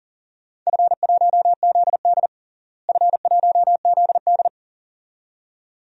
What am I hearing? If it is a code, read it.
Morse code, heard as a keyed tone, about 37 wpm, 700 Hz. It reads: F1ZD F1ZD
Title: DXCC-100 Call Signs - 4 Characters - Repeated 2x - 40wpm